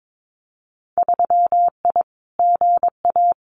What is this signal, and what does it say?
Morse code, heard as a keyed tone, about 22 wpm, 700 Hz. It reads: 3I GA
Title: Winter Field Day Exchanges 22wpm